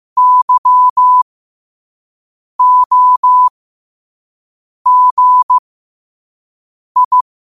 Morse code, heard as a keyed tone, about 15 wpm, spaced as for 5 wpm, 1000 Hz. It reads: YOGI
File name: qotdfiles / 5wpm